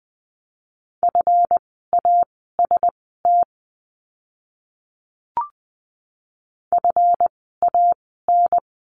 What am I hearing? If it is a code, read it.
Morse code, heard as a keyed tone, about 20 wpm, spaced as for 15 wpm, 700 Hz. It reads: FAST FAN
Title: 139 Lesson 23f - Letter f - Words 20wpm